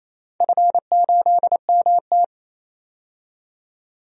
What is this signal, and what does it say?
Morse code, heard as a keyed tone, about 28 wpm, 700 Hz. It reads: F8MT
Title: DXCC-100 Call Signs - 4 Characters - 28wpm